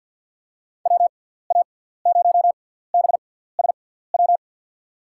Morse code, heard as a keyed tone, about 53 wpm, spaced as for 17 wpm, 700 Hz. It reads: WA0BSW
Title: USA Call Signs - 6 Characters - 6x Character Spacing - 50wpm